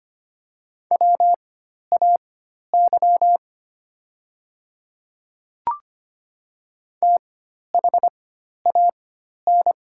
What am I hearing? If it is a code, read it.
Morse code, heard as a keyed tone, about 25 wpm, spaced as for 12 wpm, 700 Hz. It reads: WAY THAN
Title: 4x Character Spacing - Top 100 Words - 25wpm